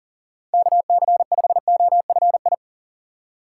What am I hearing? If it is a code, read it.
Morse code, heard as a keyed tone, about 37 wpm, 700 Hz. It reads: KC5OFI
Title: USA Call Signs - 6 Characters - 4x Character Spacing - 40wpm